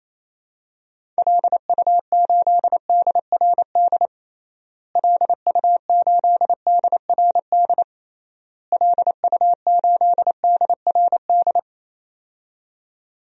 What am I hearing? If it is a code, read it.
Morse code, heard as a keyed tone, about 28 wpm, 700 Hz. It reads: LU8DRD LU8DRD LU8DRD
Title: DXCC-100 Call Signs - 6 Characters - Repeated 3x - 28wpm